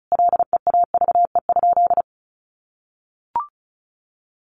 Morse code, heard as a keyed tone, about 34 wpm, 700 Hz. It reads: LEAVE?
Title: Sentences from Top 400 Words 35wpm